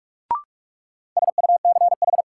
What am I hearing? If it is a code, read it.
Morse code, heard as a keyed tone, about 46 wpm, 700 Hz. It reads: SUCH
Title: Sentences from Top 200 Words - New Word Drill - 45wpm